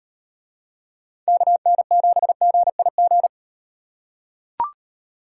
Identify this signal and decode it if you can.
Morse code, heard as a keyed tone, about 38 wpm, 700 Hz. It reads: KN7GIG